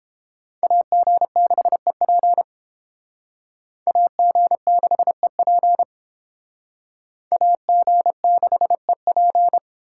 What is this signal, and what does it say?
Morse code, heard as a keyed tone, about 31 wpm, 700 Hz. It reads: AG6EP AG6EP AG6EP